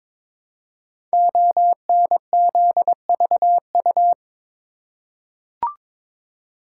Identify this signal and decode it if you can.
Morse code, heard as a keyed tone, about 22 wpm, 700 Hz. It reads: ONZVU
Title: Code Groups - 5 Letter:Numbers - 22wpm